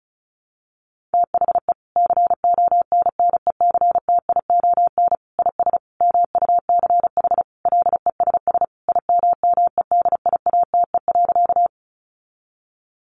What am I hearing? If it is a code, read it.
Morse code, heard as a keyed tone, about 34 wpm, 700 Hz. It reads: THE CONNECTION IS MUCH LESS IMMEDIATE.
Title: Sentences from Top 1300 Words 35wpm